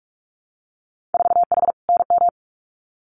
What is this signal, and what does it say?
Morse code, heard as a keyed tone, about 46 wpm, 700 Hz. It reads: 4H NM
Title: Winter Field Day Exchanges Encore 45wpm